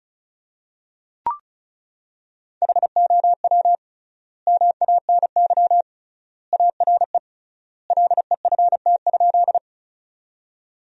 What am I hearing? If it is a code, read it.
Morse code, heard as a keyed tone, about 36 wpm, 700 Hz. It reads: HOW MANY ARE LEFT?
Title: 3x Word-Spacing - Sentences from Top 300 Words - 35wpm